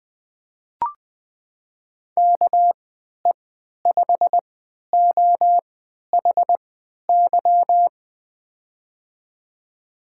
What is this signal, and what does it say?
Morse code, heard as a keyed tone, about 20 wpm, spaced as for 11 wpm, 700 Hz. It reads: KE5OHY
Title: USA Call Signs - 6 Characters - 3x Character Spacing - 20wpm